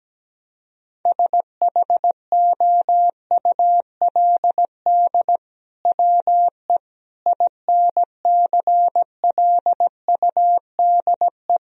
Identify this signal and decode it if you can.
Morse code, heard as a keyed tone, about 17 wpm, 700 Hz. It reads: SHOULD WE INCLUDE
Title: Sentences from Top 1000 Words - Review Part 3 - 17wpm